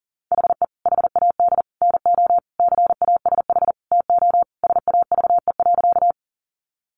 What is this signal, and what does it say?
Morse code, heard as a keyed tone, about 40 wpm, 700 Hz. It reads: HE HAD NO CASH TO SAVE.
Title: Sentences from Top 1100 Words 40wpm